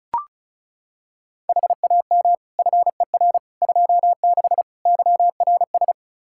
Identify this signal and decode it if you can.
Morse code, heard as a keyed tone, about 36 wpm, 700 Hz. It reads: HAM FER 26 YRS
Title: Practice QSO - Exchange 3 - More Info - 35wpm